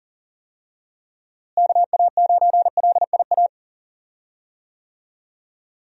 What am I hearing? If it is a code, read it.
Morse code, heard as a keyed tone, about 40 wpm, 700 Hz. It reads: KA9LIA
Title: USA Call Signs - 6 Characters - Repeated 2x - 40wpm